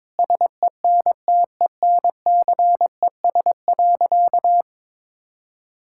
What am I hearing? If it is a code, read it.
Morse code, heard as a keyed tone, about 22 wpm, 700 Hz. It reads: SENTENCES.
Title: Pangram Challenge v2 - 22wpm